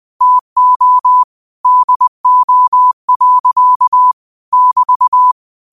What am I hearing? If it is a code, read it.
Morse code, heard as a keyed tone, about 20 wpm, 1000 Hz. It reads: TO DO. <BT>